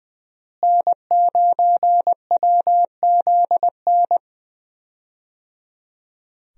Morse code, heard as a keyed tone, about 19 wpm, 700 Hz. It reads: N9WZN
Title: USA Call Signs - 5 Characters - Repeated 3x - 20wpm